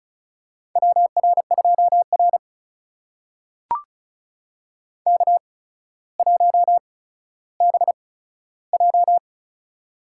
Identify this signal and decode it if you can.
Morse code, heard as a keyed tone, about 36 wpm, spaced as for 15 wpm, 700 Hz. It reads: WR2R K1BJ
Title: USA Call Signs - 4 Characters - 8x Character Spacing - 35wpm